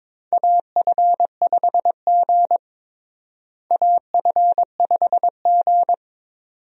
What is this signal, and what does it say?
Morse code, heard as a keyed tone, about 22 wpm, 700 Hz. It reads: AF5G AF5G
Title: USA Call Signs - 4 Characters - Repeated 2x - 22wpm